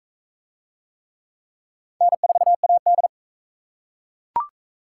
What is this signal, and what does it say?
Morse code, heard as a keyed tone, about 37 wpm, 700 Hz. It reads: N4AD